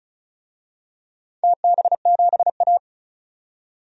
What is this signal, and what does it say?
Morse code, heard as a keyed tone, about 36 wpm, 700 Hz. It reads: TB7A
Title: DXCC-100 Call Signs - 4 Characters - 35wpm